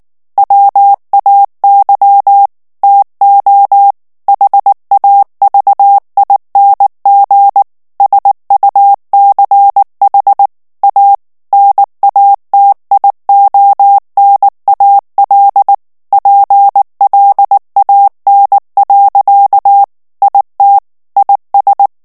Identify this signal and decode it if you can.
Morse code, heard as a keyed tone, about 19 wpm, 800 Hz. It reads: WAY TO HAVING SUCH A NATIONAL PLAN. IT IS
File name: morse mp3 / ANGLAIS texte clair 2 vitesse 18 mots minute